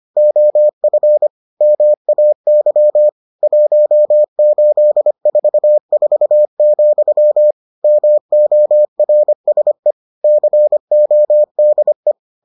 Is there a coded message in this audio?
Yes: Morse code from 0.2 to 12.1 s, heard as a keyed tone, about 25 wpm, 600 Hz. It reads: OF MAY 1844, MORSE CODE